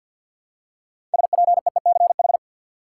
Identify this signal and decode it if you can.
Morse code, heard as a keyed tone, about 52 wpm, 700 Hz. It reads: SPEECH
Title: Six Letter Words - 50wpm